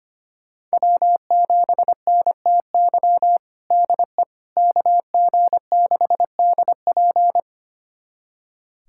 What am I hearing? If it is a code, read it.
Morse code, heard as a keyed tone, about 25 wpm, 700 Hz. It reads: W7NTY DE KG6DP
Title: USA Callsign DE Callsign - 5 Characters - 25wpm